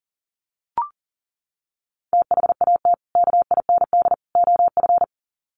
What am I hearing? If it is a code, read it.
Morse code, heard as a keyed tone, about 40 wpm, 700 Hz. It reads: THAT KIND OF
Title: Sets of 3 Words 40wpm